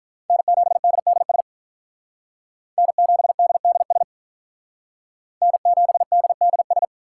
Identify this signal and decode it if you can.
Morse code, heard as a keyed tone, about 46 wpm, 700 Hz. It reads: N7DDS N7DDS N7DDS